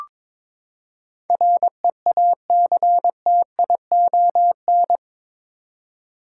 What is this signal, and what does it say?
Morse code, heard as a keyed tone, about 22 wpm, 700 Hz. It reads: REACTION
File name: practice / 22wpm / Eight Letter Words - 22wpm